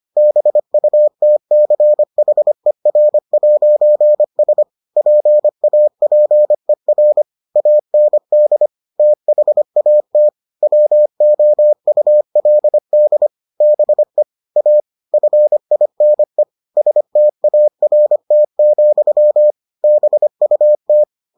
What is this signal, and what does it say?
Morse code, heard as a keyed tone, about 25 wpm, 600 Hz. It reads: BUTCHER'S PAPER AND THAT WOULD BE A FINE START, BUT